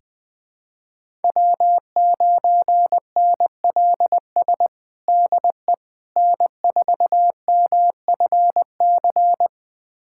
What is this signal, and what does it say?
Morse code, heard as a keyed tone, about 21 wpm, 700 Hz. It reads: W9NLS DE N4MFC
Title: USA Callsign DE Callsign 20wpm